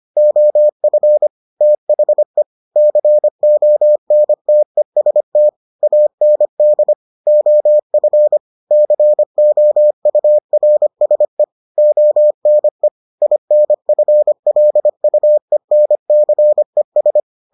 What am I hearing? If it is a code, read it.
Morse code, heard as a keyed tone, about 25 wpm, 600 Hz. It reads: OF THE CONTEST AND OF COURSE ONE INFLUENCES